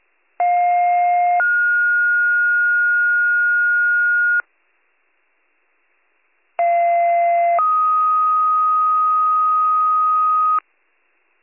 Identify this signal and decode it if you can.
Two-tone page, heard as a one-second tone, then a three-second tone, about 710 then 1430 Hz, about 690 then 1250 Hz. It is heard from 0.4 to 10.6 s.